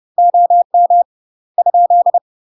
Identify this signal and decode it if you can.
Morse code, heard as a keyed tone, about 30 wpm, 700 Hz. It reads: OM ?
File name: UK MASTER QSOs TRAINING/30WPM